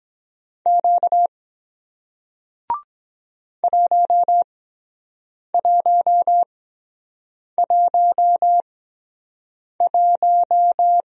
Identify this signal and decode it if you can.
Morse code, heard as a keyed tone, about 23 wpm, spaced as for 7 wpm, 700 Hz. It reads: Q 1111